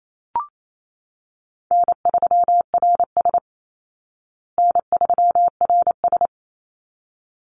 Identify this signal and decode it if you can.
Morse code, heard as a keyed tone, about 28 wpm, 700 Hz. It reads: N3RS N3RS
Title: Call Signs Repeated 3x - 28wpm